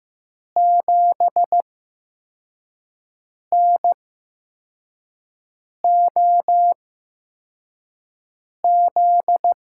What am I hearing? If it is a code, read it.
Morse code, heard as a keyed tone, about 15 wpm, spaced as for 4 wpm, 700 Hz. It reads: 7NOZ
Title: USA Call Signs - 6 Characters - 8x Character Spacing - 15wpm